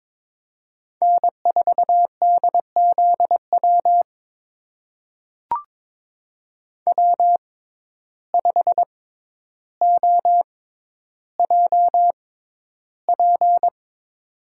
Morse code, heard as a keyed tone, about 22 wpm, spaced as for 11 wpm, 700 Hz. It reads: N4DZW W5OJP
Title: USA Call Signs - 5 Characters - 6x Character Spacing - 22wpm